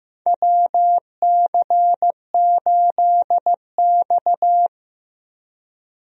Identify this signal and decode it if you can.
Morse code, heard as a keyed tone, about 15 wpm, 700 Hz. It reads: WC8X
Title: USA Call Signs - 4 Characters - Repeated 3x - 15wpm